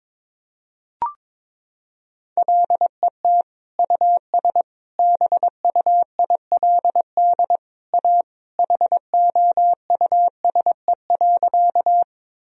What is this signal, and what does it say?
Morse code, heard as a keyed tone, about 22 wpm, 700 Hz. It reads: LET US BUILD A HOUSE.
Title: Sentences from Top 1000 Words - Review Part 1 - 22wpm